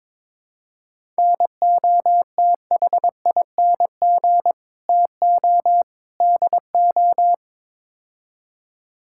Morse code, heard as a keyed tone, about 22 wpm, 700 Hz. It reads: NOTHING TO DO
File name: practice / 22wpm / Build A Phrase - 5-Word Phrases - 22wpm